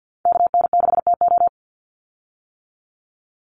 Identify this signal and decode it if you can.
Morse code, heard as a keyed tone, about 50 wpm, 700 Hz. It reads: KN6TO
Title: USA Call Signs 50wpm